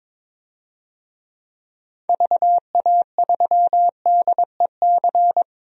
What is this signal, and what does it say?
Morse code, heard as a keyed tone, about 22 wpm, 700 Hz. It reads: VA3DEC
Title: DXCC-100 Call Signs - 6 Characters - 22wpm